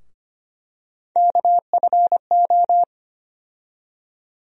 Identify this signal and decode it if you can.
Morse code, heard as a keyed tone, about 25 wpm, 700 Hz. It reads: KFO
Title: USA Call Sign Postfixes - 3 Letters 25wpm